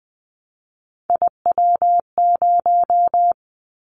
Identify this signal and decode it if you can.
Morse code, heard as a keyed tone, about 20 wpm, 700 Hz. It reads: IW0
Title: Callsign Prefixes for Top 50 DXCC - 20wpm